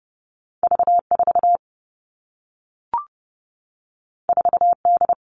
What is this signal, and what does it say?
Morse code, heard as a keyed tone, about 30 wpm, 700 Hz. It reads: V4 4D
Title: All Cut Numbers - 2-Digit 30wpm